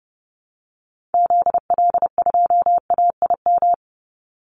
Morse code, heard as a keyed tone, about 30 wpm, 700 Hz. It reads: ZL2AIM
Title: Call Signs Repeated 2x - 30wpm